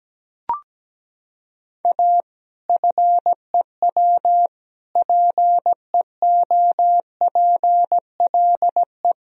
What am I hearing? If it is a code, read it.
Morse code, heard as a keyed tone, about 17 wpm, 700 Hz. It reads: A FEW PEOPLE
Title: Sentences from Top 1000 Words - Review Part 3 - 17wpm